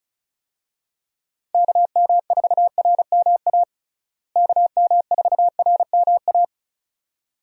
Morse code, heard as a keyed tone, about 36 wpm, 700 Hz. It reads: KM4RMA KM4RMA
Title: USA Call Signs - 6 Characters - Repeated 2x - 35wpm